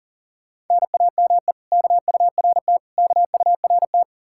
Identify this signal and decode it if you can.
Morse code, heard as a keyed tone, about 49 wpm, 700 Hz. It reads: NAME KURT KURT
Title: Speed-Racing Common QSO Phrases Encore - 40wpm